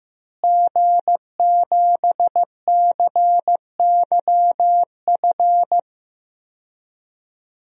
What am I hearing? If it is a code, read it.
Morse code, heard as a keyed tone, about 15 wpm, 700 Hz. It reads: G7CYF
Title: DXCC-100 Call Signs - 5 Characters - Repeated 2x - 15wpm